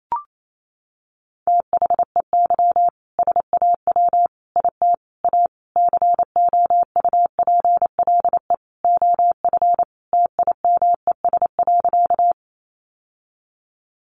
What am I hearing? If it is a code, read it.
Morse code, heard as a keyed tone, about 28 wpm, 700 Hz. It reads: THEY SAW IT A COUPLE OF TIMES.
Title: Sentences from Top 1100 Words 28wpm